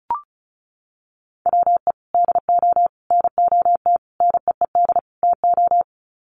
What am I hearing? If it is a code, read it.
Morse code, heard as a keyed tone, about 34 wpm, 700 Hz. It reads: WE DO NOT NEED TO
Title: Sets of 5 Words Encore 35wpm